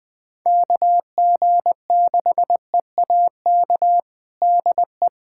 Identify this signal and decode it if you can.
Morse code, heard as a keyed tone, about 21 wpm, 700 Hz. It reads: KG6EAK DE
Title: USA Callsign DE Callsign - 6 Characters - 20wpm